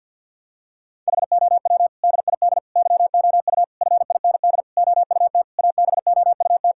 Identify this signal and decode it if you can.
Morse code, heard as a keyed tone, about 51 wpm, 700 Hz. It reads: HOW DID YOU FIND OUT ABOUT
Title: Sentences from Top 1000 Words - Review Part 5 - 50wpm